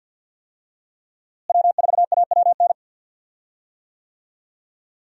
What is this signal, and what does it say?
Morse code, heard as a keyed tone, about 52 wpm, 700 Hz. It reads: W4AWN
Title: USA Call Signs - 5 Characters - Repeated 3x - 50wpm